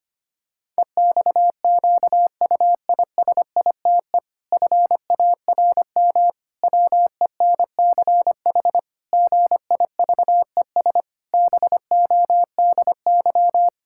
Morse code, heard as a keyed tone, about 25 wpm, 700 Hz. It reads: EXQUISITE FARM WENCH GIVES BODY
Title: Pangram Challenge v2 - 25wpm